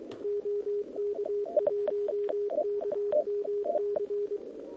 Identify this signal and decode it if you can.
Morse code, heard as a keyed tone, about 23 wpm, 440 Hz. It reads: OK1KMC
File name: OK QRP 2026 / OKQRP2026